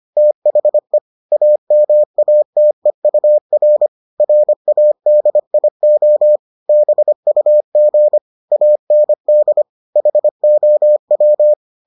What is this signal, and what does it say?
Morse code, heard as a keyed tone, about 25 wpm, 600 Hz. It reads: THE AMATEUR RADIO BUG AND HOW